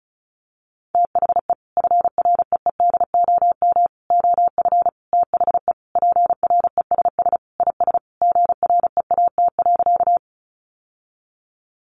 Morse code, heard as a keyed tone, about 34 wpm, 700 Hz. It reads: THE FREEDOM OF THE PRESS IS GREAT.
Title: Sentences from Top 800 Words 35wpm